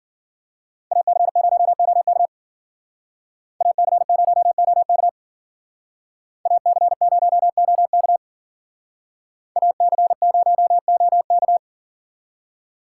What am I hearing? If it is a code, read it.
Morse code, heard as a keyed tone, about 52 wpm, 700 Hz. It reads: AC0OK AC0OK AC0OK AC0OK